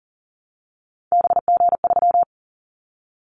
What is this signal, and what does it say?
Morse code, heard as a keyed tone, about 40 wpm, 700 Hz. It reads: BG3
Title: Callsign Prefixes for Top 100 DXCC - 40wpm